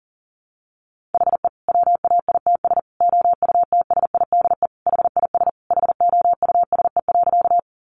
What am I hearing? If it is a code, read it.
Morse code, heard as a keyed tone, about 40 wpm, 700 Hz. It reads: HE WAITS OUTSIDE HIS HOUSE.
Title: Additional Easy Phrases 40wpm